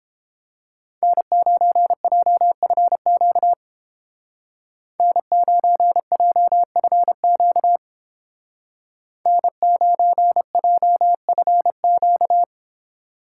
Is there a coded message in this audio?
Yes: Morse code from 1.0 to 12.4 s, heard as a keyed tone, about 31 wpm, 700 Hz. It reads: N9JFQ N9JFQ N9JFQ